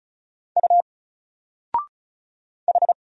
Morse code, heard as a keyed tone, about 36 wpm, 700 Hz. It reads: U H
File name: Single-Letter-Number-35wpm